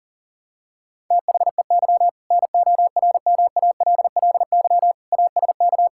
Morse code, heard as a keyed tone, about 40 wpm, 700 Hz. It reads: THEY NORMALLY ASK
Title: Sentences from Top 1000 Words - Review Part 6 - 40wpm